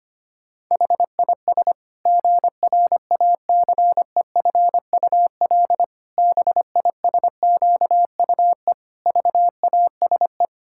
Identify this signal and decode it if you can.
Morse code, heard as a keyed tone, about 25 wpm, 700 Hz. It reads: HIS GRACEFUL BISQUE VASE
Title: Pangram Challenge v2 - 25wpm